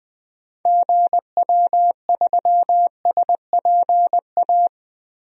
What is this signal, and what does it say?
Morse code, heard as a keyed tone, about 20 wpm, 700 Hz. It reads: GW3SPA
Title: DXCC-100 Call Signs - 6 Characters - 20wpm